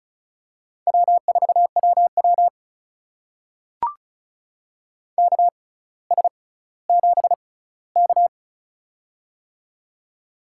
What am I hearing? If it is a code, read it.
Morse code, heard as a keyed tone, about 36 wpm, spaced as for 18 wpm, 700 Hz. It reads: W4WW KS7K
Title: USA Call Signs - 4 Characters - 6x Character Spacing - 35wpm